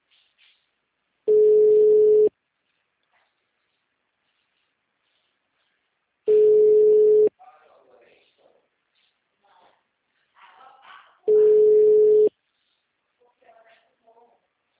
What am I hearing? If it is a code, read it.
Ringback tone (Europe, 425 Hz), with speech.
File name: out- -2002- - - .30